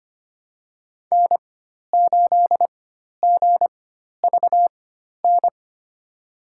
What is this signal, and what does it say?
Morse code, heard as a keyed tone, about 25 wpm, spaced as for 12 wpm, 700 Hz. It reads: N8GVN
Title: USA Call Signs - 5 Characters - 4x Character Spacing - 25wpm